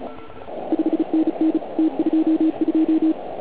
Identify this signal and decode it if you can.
Morse code, heard as a keyed tone, about 36 wpm, 320 Hz. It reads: 5NN T22
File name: Archiv / CUC129 / CUC129 pileup